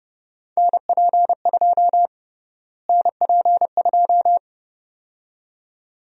Morse code, heard as a keyed tone, about 30 wpm, 700 Hz. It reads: NP2 NP2
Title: USA Call Sign Prefixes - Repeated 2x - 30wpm